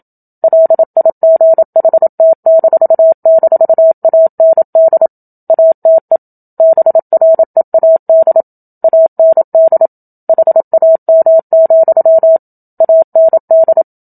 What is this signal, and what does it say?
Morse code, heard as a keyed tone, about 27 wpm, 650 Hz. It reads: LIGHT--AND ATE BREAD AND HAM, AND
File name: Book Two / 01-Chapter 1 - 27 WPM